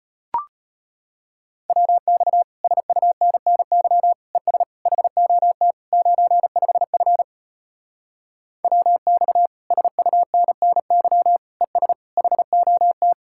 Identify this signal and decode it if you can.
Morse code, heard as a keyed tone, about 33 wpm, 700 Hz. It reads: WX SUNNY ES HOT 95F WX SUNNY ES HOT